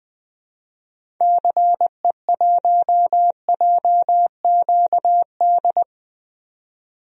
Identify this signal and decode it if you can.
Morse code, heard as a keyed tone, about 20 wpm, 700 Hz. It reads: CE1JQD
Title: DXCC-100 Call Signs - 6 Characters - 20wpm